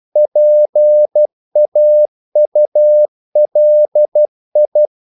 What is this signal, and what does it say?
Morse code, heard as a keyed tone, about 12 wpm, 600 Hz. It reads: PAULI
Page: English quotations in CW (telegraphy - morse)